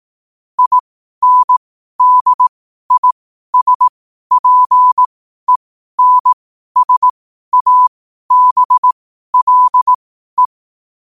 Morse code, heard as a keyed tone, about 18 wpm, spaced as for 12 wpm, 1000 Hz. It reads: INDISPENSABLE
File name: qotdfiles / 12wpm